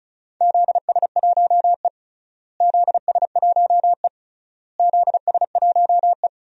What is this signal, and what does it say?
Morse code, heard as a keyed tone, about 36 wpm, 700 Hz. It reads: ZS1E ZS1E ZS1E
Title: DXCC-100 Call Signs - 4 Characters - Repeated 3x - 35wpm